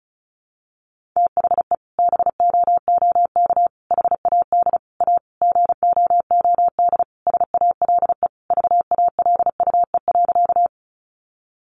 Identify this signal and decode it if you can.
Morse code, heard as a keyed tone, about 34 wpm, 700 Hz. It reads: THE BOOK HAD A GOOD SALE VALUE.
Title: Sentences from Top 1400 Words 35wpm